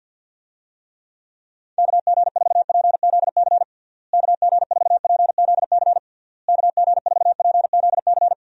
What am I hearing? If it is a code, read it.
Morse code, heard as a keyed tone, about 49 wpm, 700 Hz. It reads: KG4PZC KG4PZC KG4PZC
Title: USA Call Signs - 6 Characters - Repeated 3x - 50wpm